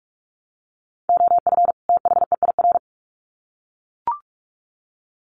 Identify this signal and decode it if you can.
Morse code, heard as a keyed tone, about 44 wpm, 700 Hz. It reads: OF THEIR
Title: Sets of 2 Words 45wpm